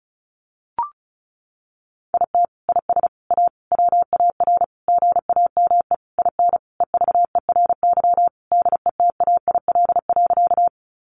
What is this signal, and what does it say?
Morse code, heard as a keyed tone, about 34 wpm, 700 Hz. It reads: IT IS A WAR GAME IN EVERY DETAIL.
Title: Sentences from Top 1500 Words 35wpm